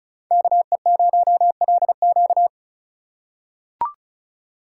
Morse code, heard as a keyed tone, about 36 wpm, 700 Hz. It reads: KE0LQ
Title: USA Call Signs - 4x Character Spacing - 35wpm